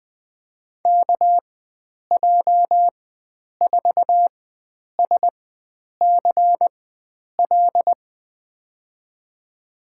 Morse code, heard as a keyed tone, about 20 wpm, spaced as for 9 wpm, 700 Hz. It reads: KJ4SCL
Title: USA Call Signs - 6 Characters - 4x Character Spacing - 20wpm